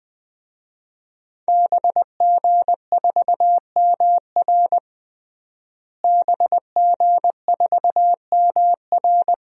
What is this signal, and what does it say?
Morse code, heard as a keyed tone, about 20 wpm, 700 Hz. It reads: BG4MR BG4MR
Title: DXCC-100 Call Signs - 5 Characters - Repeated 2x - 20wpm